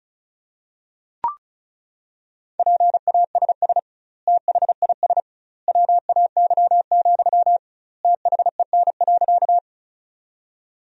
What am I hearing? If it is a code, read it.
Morse code, heard as a keyed tone, about 34 wpm, 700 Hz. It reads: PASS THIS WAY, THEN.
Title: 2x Word-Spacing - Sentences from Top 400 Words - 35wpm